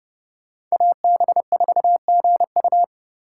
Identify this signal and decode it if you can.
Morse code, heard as a keyed tone, about 30 wpm, 700 Hz. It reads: AB4GU
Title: USA Call Signs - 5 Characters - 4x Character Spacing - 30wpm